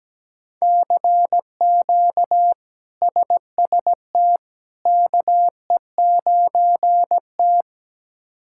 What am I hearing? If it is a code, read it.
Morse code, heard as a keyed tone, about 17 wpm, 700 Hz. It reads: CQ SST KE9T
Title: K1USN SST QSO Practice - 17wpm